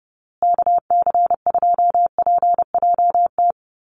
Morse code, heard as a keyed tone, about 30 wpm, 700 Hz. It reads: KC2PJT
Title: USA Call Signs Repeated 2x 30wpm